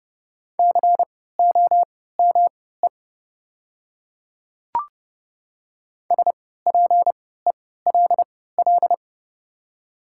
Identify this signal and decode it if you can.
Morse code, heard as a keyed tone, about 31 wpm, spaced as for 17 wpm, 700 Hz. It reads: COME SPELL
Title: 3x Character Spacing - Top 300 Words - 30wpm